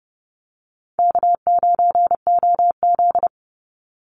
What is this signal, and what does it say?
Morse code, heard as a keyed tone, about 30 wpm, 700 Hz. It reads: K9OZ
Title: NA SOTA Chaser Callsigns 30wpm